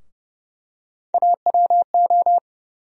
Morse code, heard as a keyed tone, about 31 wpm, 700 Hz. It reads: AWO
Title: USA Call Sign Postfixes - 3 Letters 30wpm